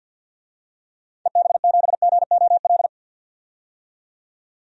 Morse code, heard as a keyed tone, about 47 wpm, 700 Hz. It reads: EB7GOL
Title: DXCC-100 Call Signs - 6 Characters - 50wpm